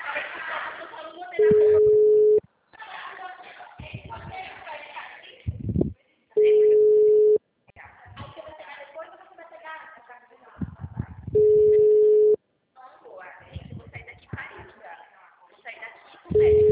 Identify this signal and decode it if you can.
Ringback tone (Europe, 425 Hz), with speech.
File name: out- -2002- - - .66757